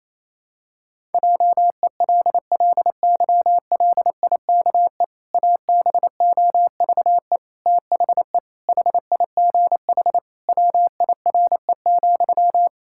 Morse code, heard as a keyed tone, about 28 wpm, 700 Hz. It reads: JELLYLIKE ABOVE THE HIGH WIRE,
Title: Pangram Challenge v2 - 28wpm